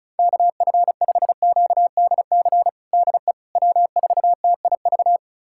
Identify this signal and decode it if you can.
Morse code, heard as a keyed tone, about 34 wpm, 700 Hz. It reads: KF5QDC DE W4TIV
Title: USA Callsign DE Callsign 35wpm